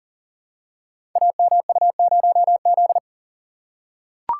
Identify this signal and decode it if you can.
Morse code, heard as a keyed tone, about 39 wpm, 700 Hz. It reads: AMU0Z
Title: Code Groups - 5 Letter:Numbers - 40wpm